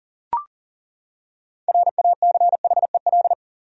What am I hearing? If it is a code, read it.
Morse code, heard as a keyed tone, about 40 wpm, 700 Hz. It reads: RACHEL
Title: USA Common Names - 40wpm